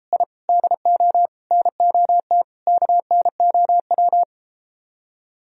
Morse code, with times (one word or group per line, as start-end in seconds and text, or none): 0.1-0.2 I
0.5-1.3 DO
1.5-2.4 NOT
2.7-4.2 KNOW